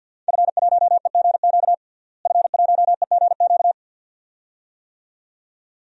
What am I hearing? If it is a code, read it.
Morse code, heard as a keyed tone, about 47 wpm, 700 Hz. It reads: F1EGQ F1EGQ
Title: DXCC-100 Call Signs - 5 Characters - Repeated 2x - 50wpm